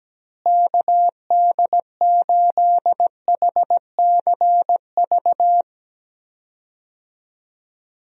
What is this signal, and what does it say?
Morse code, heard as a keyed tone, about 17 wpm, 700 Hz. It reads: KD8HCV
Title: USA Call Signs - 6 Characters - Repeated 2x - 17wpm